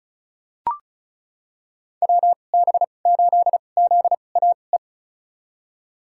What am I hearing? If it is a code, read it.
Morse code, heard as a keyed tone, about 34 wpm, spaced as for 25 wpm, 700 Hz. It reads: WB8ZAE
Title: USA Call Signs - 2x Character Spacing - 35wpm